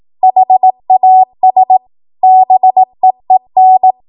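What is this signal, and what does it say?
Morse code, heard as a keyed tone, about 18 wpm, 750 Hz. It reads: HAS BEEN
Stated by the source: 18 WPM Morse code practice from a national amateur radio association's code archive.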